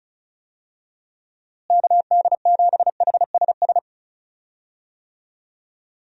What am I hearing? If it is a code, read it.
Morse code, heard as a keyed tone, about 36 wpm, 700 Hz. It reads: KD7HSS
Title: USA Call Signs - 6 Characters - Repeated 2x - 35wpm